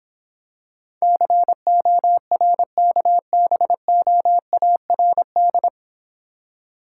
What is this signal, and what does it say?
Morse code, heard as a keyed tone, about 26 wpm, 700 Hz. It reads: CORKBOARD